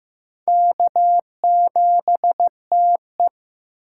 Morse code, heard as a keyed tone, about 15 wpm, 700 Hz. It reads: K7TE
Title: USA Call Signs - 4 Characters - 6x Character Spacing - 15wpm